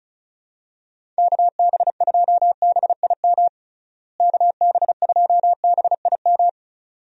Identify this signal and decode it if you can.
Morse code, heard as a keyed tone, about 34 wpm, 700 Hz. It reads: KB2BIM KB2BIM
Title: USA Call Signs - 6 Characters - Repeated 2x - 35wpm